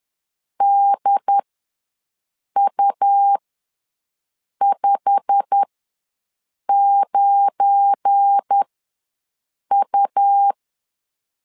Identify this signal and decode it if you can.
Morse code, heard as a keyed tone, about 10 wpm, spaced as for 6 wpm, 800 Hz. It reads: DU59U
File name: LEZIONI / 10 SIGNS / 10A SLASH ws 10wpm 4121